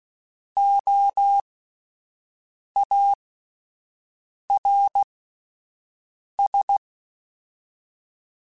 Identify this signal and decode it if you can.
Morse code, heard as a keyed tone, about 16 wpm, spaced as for 5 wpm, 800 Hz. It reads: OARS
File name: Alphabetical / Alphabetical mp3s / Os @5